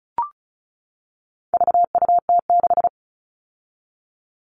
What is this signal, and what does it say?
Morse code, heard as a keyed tone, about 34 wpm, 700 Hz. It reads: VUT6
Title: All Cut Numbers - 4-Digit 35wpm